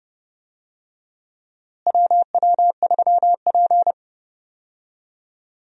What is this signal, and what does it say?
Morse code, heard as a keyed tone, about 30 wpm, 700 Hz. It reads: WW3P
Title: USA Call Signs - 4 Characters - Repeated 3x - 30wpm